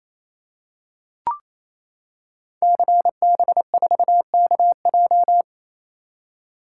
Morse code, heard as a keyed tone, about 28 wpm, 700 Hz. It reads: CB4KJ
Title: Code Groups - 5 Letter:Numbers - 28wpm